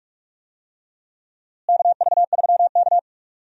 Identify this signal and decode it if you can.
Morse code, heard as a keyed tone, about 42 wpm, 700 Hz. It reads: KV3K
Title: USA Call Signs - 4 Characters - Repeated 2x - 45wpm